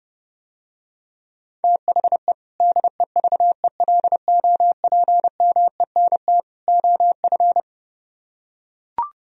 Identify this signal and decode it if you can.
Morse code, heard as a keyed tone, about 30 wpm, 700 Hz. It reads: THE DEVELOPMENT OF